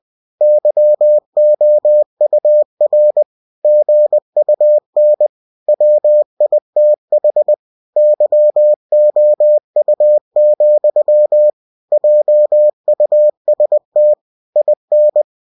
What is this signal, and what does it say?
Morse code, heard as a keyed tone, about 20 wpm, 600 Hz. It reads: YOUR GUN WITH YOU, JUST IN
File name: cw-books / winnie-the-pooh / winnie-20wpm-page-0006